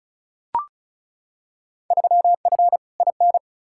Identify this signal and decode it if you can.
Morse code, heard as a keyed tone, about 36 wpm, 700 Hz. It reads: 3F IN
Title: Field Day Exchanges - 35wpm